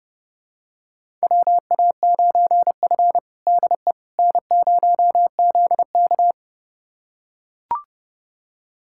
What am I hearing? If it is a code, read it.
Morse code, heard as a keyed tone, about 30 wpm, 700 Hz. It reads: WA9F DE N0ZK
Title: USA Callsign DE Callsign - 4 Characters - 30wpm